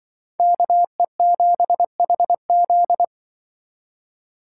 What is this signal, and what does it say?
Morse code, heard as a keyed tone, about 24 wpm, 700 Hz. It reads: KE7HZ
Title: Speed Racing - USA 2x2 Call Signs 20wpm